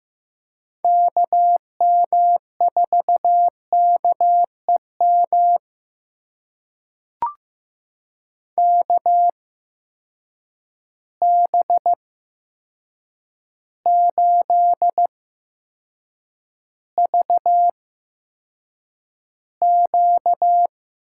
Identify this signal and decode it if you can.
Morse code, heard as a keyed tone, about 15 wpm, spaced as for 7 wpm, 700 Hz. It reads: KM4KEM KB8VQ
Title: USA Call Signs - 6 Characters - 8x Character Spacing - 15wpm